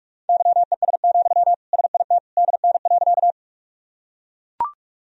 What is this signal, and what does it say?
Morse code, heard as a keyed tone, about 41 wpm, 700 Hz. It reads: YES, SIT DN.
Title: Sentences from Top 500 Words and Abbreviations 45wpm